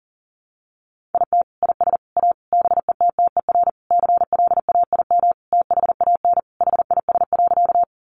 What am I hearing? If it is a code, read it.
Morse code, heard as a keyed tone, about 40 wpm, 700 Hz. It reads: IT IS A BETTER CLAIM THAN HIS.
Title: Sentences from Top 1500 Words 40wpm